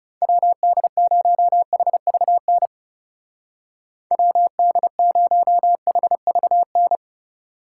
Morse code, heard as a keyed tone, about 36 wpm, 700 Hz. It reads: WD0HVN WD0HVN
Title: Speed Racing - USA Call Signs 30wpm